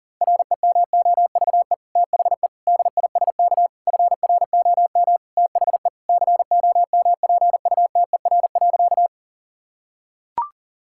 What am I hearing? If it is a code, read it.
Morse code, heard as a keyed tone, about 40 wpm, 700 Hz. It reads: REMOVE THE DISK FROM THE COMPUTER.
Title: Sentences from Top 1000 Words - Review Part 6 - 40wpm